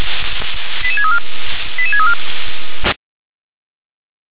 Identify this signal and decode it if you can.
Five-tone selective call (ZVEI), heard as quick runs of five tones, tones of about 70 ms. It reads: 88634 88634
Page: /357/SDR/ZVEI/88511/2023.06.19/